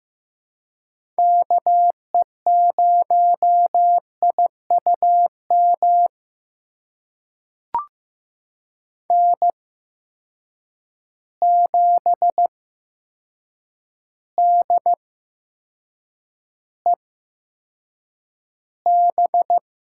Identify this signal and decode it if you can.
Morse code, heard as a keyed tone, about 15 wpm, spaced as for 7 wpm, 700 Hz. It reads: KE0IUM N7DEB
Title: USA Call Signs - 8x Character Spacing - 15wpm